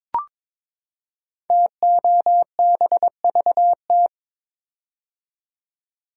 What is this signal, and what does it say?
Morse code, heard as a keyed tone, about 22 wpm, 700 Hz. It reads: TOBVT
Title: Code Groups - 5 Letters - 22wpm